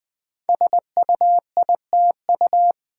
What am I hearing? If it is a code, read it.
Morse code, heard as a keyed tone, about 20 wpm, 700 Hz. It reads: SUITU
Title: Code Groups - 5 Letters - 20wpm